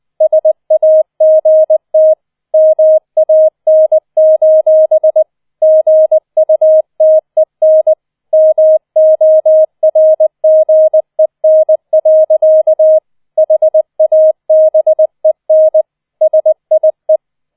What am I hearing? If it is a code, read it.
Morse code, heard as a keyed tone, about 20 wpm, 630 Hz. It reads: SAGT MAN: GUTEN MORGEN. HABEN SIE